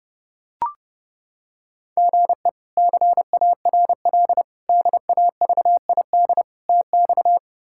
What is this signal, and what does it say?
Morse code, heard as a keyed tone, about 30 wpm, 700 Hz. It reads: GE CARL DAVID TX
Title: K1USN SST QSO Practice - 30wpm